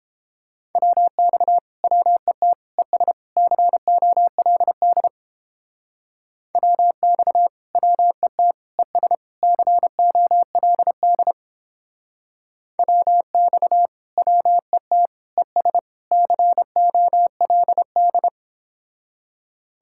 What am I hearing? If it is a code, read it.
Morse code, heard as a keyed tone, about 29 wpm, 700 Hz. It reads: WX WET ES COLD WX WET ES COLD WX WET ES COLD